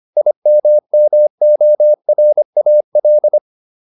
Morse code, heard as a keyed tone, about 25 wpm, 600 Hz. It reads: IMMORAL